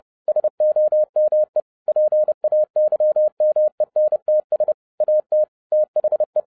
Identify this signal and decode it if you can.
Morse code, heard as a keyed tone, about 30 wpm, 610 Hz. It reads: SOME PAYMENTS AT THE
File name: 2011-07-mp3 / n55 BACK FROM A BIG ACCIDENT 30wpm